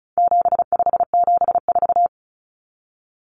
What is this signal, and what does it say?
Morse code, heard as a keyed tone, about 34 wpm, 700 Hz. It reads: 7574
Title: Common Cut Numbers - 4-Digit 35wpm